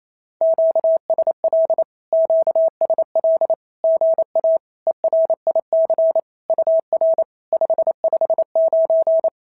Morse code, heard as a keyed tone, about 27 wpm, 650 Hz. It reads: QSL QSL GA ERIC UR 559
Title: SOTA QSO Practice 28wpm